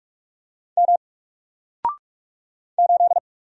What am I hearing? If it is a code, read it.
Morse code, heard as a keyed tone, about 46 wpm, 700 Hz. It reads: M 8
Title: Single Character (Letter:Number:Punctuation:Prosign) Alt 45wpm